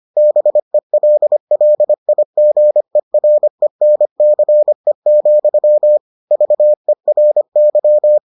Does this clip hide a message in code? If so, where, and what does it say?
Yes: Morse code from 0.2 to 8.2 s, heard as a keyed tone, about 25 wpm, 600 Hz. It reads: BELLIGERENCE, VERY